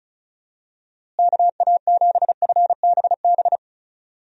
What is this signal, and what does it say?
Morse code, heard as a keyed tone, about 36 wpm, 700 Hz. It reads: KA7FBB
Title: USA Call Signs - 6 Characters - Repeated 2x - 35wpm